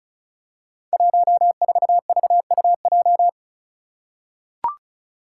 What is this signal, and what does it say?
Morse code, heard as a keyed tone, about 36 wpm, 700 Hz. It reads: 14VUJ
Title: Five Challenging Letter-Number Combinations 35wpm